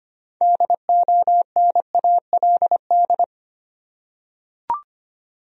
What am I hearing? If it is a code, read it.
Morse code, heard as a keyed tone, about 25 wpm, 700 Hz. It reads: DONALD
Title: USA Common Names - 25wpm